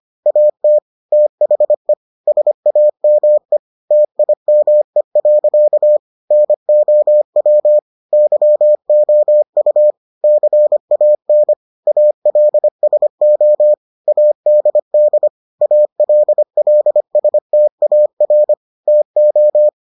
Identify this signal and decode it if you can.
Morse code, heard as a keyed tone, about 25 wpm, 600 Hz. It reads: AT THE SAME TIME. NOW YOU CAN ALSO ADD ALLSTAR TO